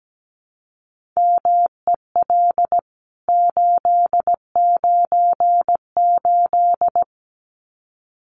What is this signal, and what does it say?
Morse code, heard as a keyed tone, about 17 wpm, 700 Hz. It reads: MEL 898
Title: CWT Exchange 17wpm